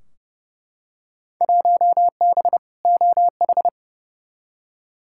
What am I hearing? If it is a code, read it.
Morse code, heard as a keyed tone, about 30 wpm, 700 Hz. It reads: 1B OH
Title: Field Day Exchanges - 30wpm